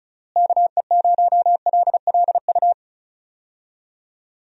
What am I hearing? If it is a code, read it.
Morse code, heard as a keyed tone, about 36 wpm, 700 Hz. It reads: KE0LLU
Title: USA Call Signs - 6 Characters - Repeated 3x - 35wpm